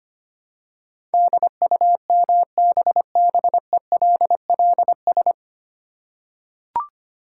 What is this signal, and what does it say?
Morse code, heard as a keyed tone, about 25 wpm, 700 Hz. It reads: DUMBBELLS
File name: practice / 25wpm / Easy Words with Hints - 25wpm